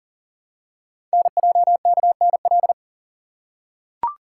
Morse code, heard as a keyed tone, about 43 wpm, 700 Hz. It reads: NJKNL
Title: Code Groups - 5 Letter:Numbers - 40wpm